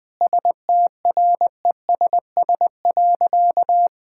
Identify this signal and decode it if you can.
Morse code, heard as a keyed tone, about 21 wpm, 700 Hz. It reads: STRESS.
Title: Difficult Sentences with Hints - 20wpm